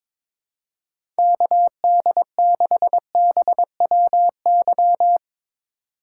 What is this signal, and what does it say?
Morse code, heard as a keyed tone, about 22 wpm, 700 Hz. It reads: KD6BWY
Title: USA Call Signs - 3x Character Spacing - 22wpm